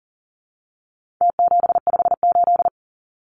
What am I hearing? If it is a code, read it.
Morse code, heard as a keyed tone, about 40 wpm, 700 Hz. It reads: T758
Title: Numbers - Contest Style - 4-Digit 40wpm